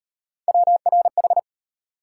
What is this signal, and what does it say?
Morse code, heard as a keyed tone, about 31 wpm, 700 Hz. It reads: WRH
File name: Speed-Racing - Three Letter-Number Combinations - 25wpm